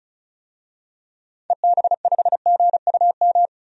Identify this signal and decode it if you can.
Morse code, heard as a keyed tone, about 34 wpm, 700 Hz. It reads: EB5GUM
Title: DXCC-100 Call Signs - 6 Characters - 35wpm